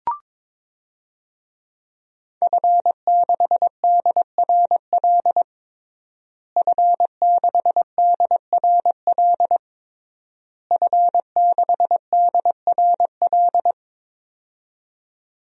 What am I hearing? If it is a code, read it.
Morse code, heard as a keyed tone, about 22 wpm, 700 Hz. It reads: F6DRL F6DRL F6DRL
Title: DXCC-100 Call Signs - 5 Characters - Repeated 3x - 22wpm